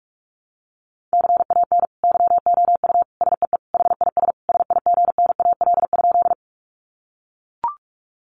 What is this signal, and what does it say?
Morse code, heard as a keyed tone, about 44 wpm, 700 Hz. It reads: CAN YOU SEE HIS SIGNAL?
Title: Sentences from Top 1400 Words 45wpm